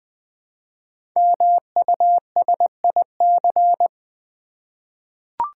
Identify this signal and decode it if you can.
Morse code, heard as a keyed tone, about 20 wpm, 700 Hz. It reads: MUSIC
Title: Easy Words with Hints - 20wpm